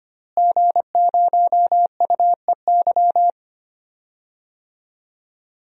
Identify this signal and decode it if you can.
Morse code, heard as a keyed tone, about 25 wpm, 700 Hz. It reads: G0UEY
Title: DXCC-100 Call Signs - 5 Characters - Repeated 2x - 25wpm